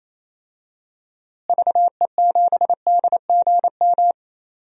Morse code, heard as a keyed tone, about 28 wpm, 700 Hz. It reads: VE7DGM
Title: DXCC-100 Call Signs - 6 Characters - Repeated 2x - 28wpm